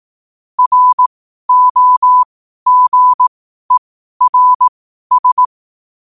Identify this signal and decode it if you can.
Morse code, heard as a keyed tone, about 18 wpm, spaced as for 12 wpm, 1000 Hz. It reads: ROGERS
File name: qotdfiles / 12wpm